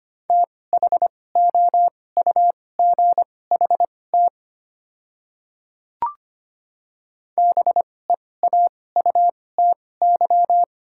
Morse code, heard as a keyed tone, about 25 wpm, spaced as for 18 wpm, 700 Hz. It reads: THOUGHT BEAUTY
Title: 2x Character Spacing - Top 500 Words - 25wpm